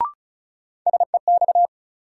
Morse code, heard as a keyed tone, about 36 wpm, 700 Hz. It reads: SEX
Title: Sentences from Top 1600 Words - New Word Drill - 35wpm